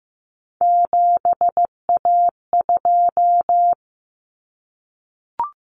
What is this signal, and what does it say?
Morse code, heard as a keyed tone, about 15 wpm, 700 Hz. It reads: 7A2
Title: Common Cut Numbers - 3-Digit 15wpm